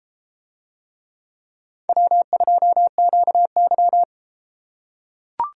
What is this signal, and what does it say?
Morse code, heard as a keyed tone, about 32 wpm, 700 Hz. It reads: W2QY